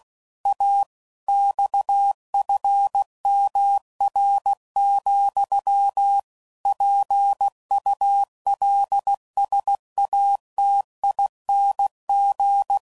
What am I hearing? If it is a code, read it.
Morse code, heard as a keyed tone, about 16 wpm, 800 Hz. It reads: A XFMR, PULSATING
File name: CW Code Groups @15WPM/Technical Subjects